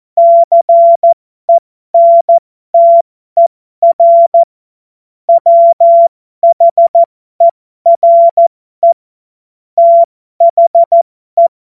Morse code, heard as a keyed tone, about 14 wpm, spaced as for 12 wpm, 680 Hz. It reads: CENTER WHERE THE
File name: Morsekurs / Witze